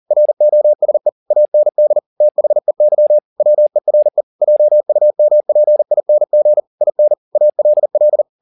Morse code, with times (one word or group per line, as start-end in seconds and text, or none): 0.1-1.1 ROSE
1.3-2.0 AND
2.2-3.2 THEY
3.4-4.2 WERE
4.4-6.6 JUMPING
6.8-7.1 IN
7.4-8.2 ALL